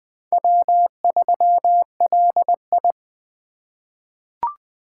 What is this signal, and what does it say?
Morse code, heard as a keyed tone, about 20 wpm, 700 Hz. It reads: W3LI
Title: USA Call Signs - 4 Characters - 4x Character Spacing - 20wpm